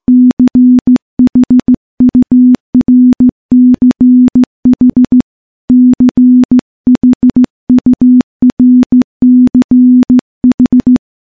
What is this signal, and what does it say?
Morse code, heard as a keyed tone, about 15 wpm, 250 Hz. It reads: CHURCH CHURCH
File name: morse / morse-training-15WPM